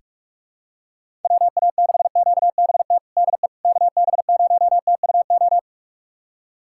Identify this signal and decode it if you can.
Morse code, heard as a keyed tone, about 46 wpm, 700 Hz. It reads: WA6QBT DE KB0TUO
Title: USA Callsign DE Callsign 45wpm